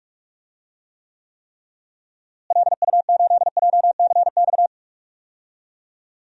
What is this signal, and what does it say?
Morse code, heard as a keyed tone, about 46 wpm, 700 Hz. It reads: LU8JCX
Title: DXCC-100 Call Signs - 6 Characters - Repeated 3x - 45wpm